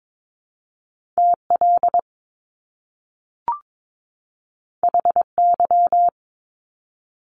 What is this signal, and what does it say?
Morse code, heard as a keyed tone, about 22 wpm, 700 Hz. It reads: TL HY
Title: Top Two Letter Combinations in Words 22wpm